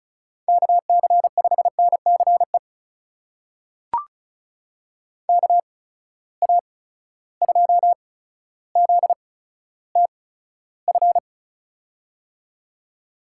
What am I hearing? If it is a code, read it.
Morse code, heard as a keyed tone, about 36 wpm, spaced as for 15 wpm, 700 Hz. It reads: KC5NCE KA2ZTF
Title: USA Call Signs - 6 Characters - 8x Character Spacing - 35wpm